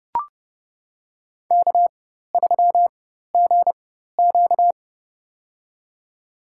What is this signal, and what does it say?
Morse code, heard as a keyed tone, about 32 wpm, spaced as for 14 wpm, 700 Hz. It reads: K3GQ
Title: USA Call Signs - 4 Characters - 4x Character Spacing - 30wpm